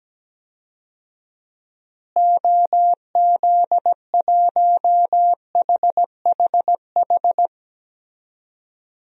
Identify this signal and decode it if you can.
Morse code, heard as a keyed tone, about 17 wpm, 700 Hz. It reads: OZ1HHH
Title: DXCC-100 Call Signs - 6 Characters - Repeated 2x - 17wpm